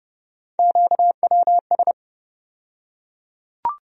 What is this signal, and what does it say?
Morse code, heard as a keyed tone, about 30 wpm, 700 Hz. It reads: QWS
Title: USA Call Sign Postfixes - 3 Letters 30wpm